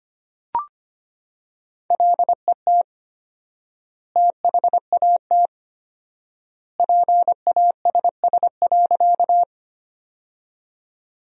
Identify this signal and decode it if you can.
Morse code, heard as a keyed tone, about 25 wpm, 700 Hz. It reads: LET THAT PASS.
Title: 4x Word-Spacing - Sentences from Top 400 Words - 25wpm